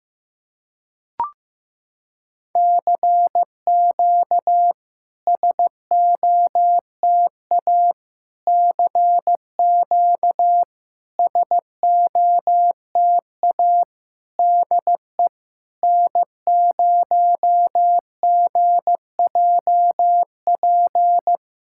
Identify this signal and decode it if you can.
Morse code, heard as a keyed tone, about 15 wpm, 690 Hz. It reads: CQ SOTA CQ SOTA DE N0GJP
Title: SOTA QSO Practice 15wpm